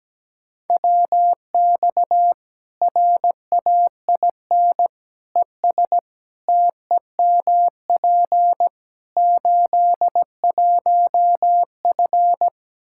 Morse code, heard as a keyed tone, about 17 wpm, 700 Hz. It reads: WX RAIN ES TEMP 81F
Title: Practice QSO - Exchange 2 - Info - 17wpm